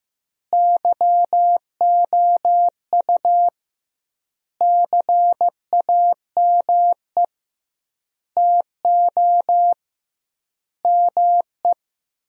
Morse code, heard as a keyed tone, about 15 wpm, 700 Hz. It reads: YOU CAME TO ME
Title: 2x Word-Spacing - Sentences from Top 200 Words - 15wpm